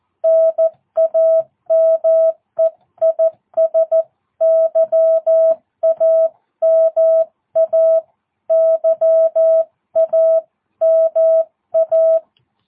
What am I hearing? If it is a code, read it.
Morse code, heard as a keyed tone, about 14 wpm, 650 Hz. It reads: NAME IS YAMA YAMA
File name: jh8cbh / 2022cw